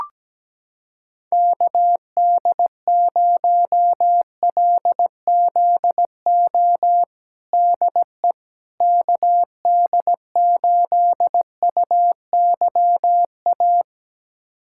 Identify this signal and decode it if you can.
Morse code, heard as a keyed tone, about 17 wpm, 700 Hz. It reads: KD0LZO DE KD8UYA
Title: USA Callsign DE Callsign - 6 Characters - 17wpm